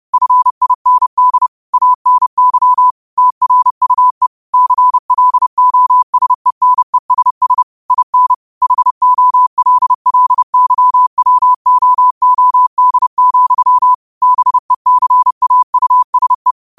Morse code, heard as a keyed tone, about 30 wpm, 1000 Hz. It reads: FIND ANY TRUE CLOSENESS IN HOLLYWOOD, BECAUSE
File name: 30wpm